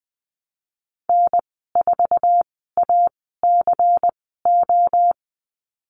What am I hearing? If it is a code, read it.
Morse code, heard as a keyed tone, about 20 wpm, spaced as for 15 wpm, 700 Hz. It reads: N4ACO
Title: 147 Lesson 24g - Letter y - Call Signs 20wpm